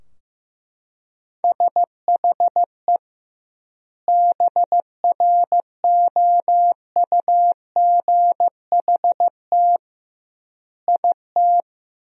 Morse code, heard as a keyed tone, about 15 wpm, 700 Hz. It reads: SHE BROUGHT IT
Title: 2x Word-Spacing - Sentences from Top 500 Words - 15wpm